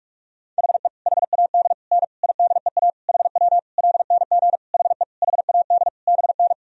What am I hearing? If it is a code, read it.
Morse code, heard as a keyed tone, about 46 wpm, 700 Hz. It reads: HE HAD N IDEA HW LNG HE HAD BN
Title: Sentences from Top 300 Words and Abbreviations 45wpm